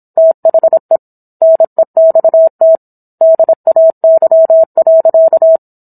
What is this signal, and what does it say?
Morse code, heard as a keyed tone, about 27 wpm, 650 Hz. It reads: THE NEXT DAY.